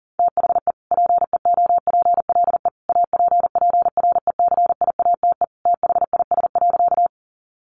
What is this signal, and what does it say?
Morse code, heard as a keyed tone, about 40 wpm, 700 Hz. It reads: THE PEOPLE APPRECIATE THIS.
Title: Sentences from Top 1300 Words 40wpm